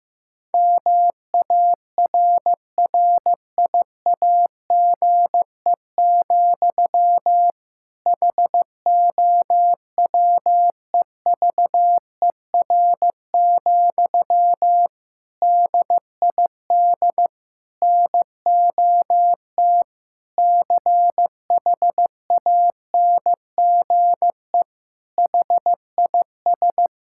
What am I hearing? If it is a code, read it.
Morse code, heard as a keyed tone, about 15 wpm, 700 Hz. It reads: MARRIAGE, HOWEVER, DID NOT CHANGE HIS
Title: Sentences from Top 1000 Words - Review Part 5 - 15wpm